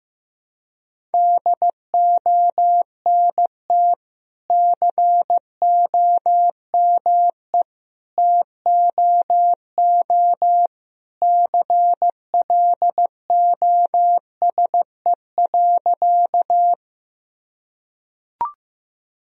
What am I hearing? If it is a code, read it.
Morse code, heard as a keyed tone, about 15 wpm, 700 Hz. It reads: DONT COME TOO CLOSE.
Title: Sentences from Top 1000 Words - Review Part 1 - 15wpm